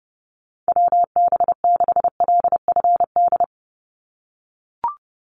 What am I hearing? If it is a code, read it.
Morse code, heard as a keyed tone, about 30 wpm, 700 Hz. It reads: WB6LFD
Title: USA Call Signs 30wpm